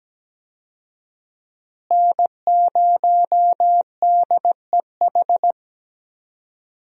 Morse code, heard as a keyed tone, about 17 wpm, 700 Hz. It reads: N0DEH
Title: USA Call Signs - 5 Characters - Repeated 3x - 17wpm